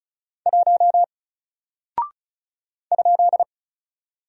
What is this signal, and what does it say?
Morse code, heard as a keyed tone, about 36 wpm, 700 Hz. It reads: 1 ?
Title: Single Character (Letter:Number:Punctuation:Prosign) 35wpm